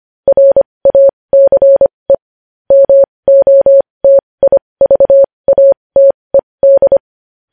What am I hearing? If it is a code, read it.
Morse code, heard as a keyed tone, about 25 wpm, spaced as for 20 wpm, 550 Hz. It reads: RACE MOTIVATED
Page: News headlines in Morse code at 20 wpm